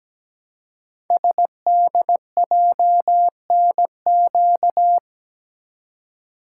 Morse code, heard as a keyed tone, about 17 wpm, 700 Hz. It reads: SDJNQ
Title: Code Groups - 5 Letter:Numbers - 17wpm